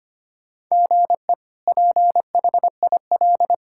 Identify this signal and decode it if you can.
Morse code, heard as a keyed tone, about 25 wpm, 700 Hz. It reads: GE PHIL
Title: K1USN SST QSO Practice - 25wpm